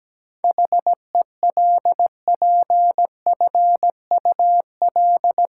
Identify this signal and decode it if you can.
Morse code, heard as a keyed tone, about 17 wpm, 700 Hz. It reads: HELPFUL
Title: Sentences from Top 1000 Words - Review Part 6 - New Word Drill - 17wpm